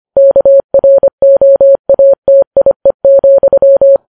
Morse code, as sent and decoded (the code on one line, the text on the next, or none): -.- .-. --- .- - .. --..--
KROATIE,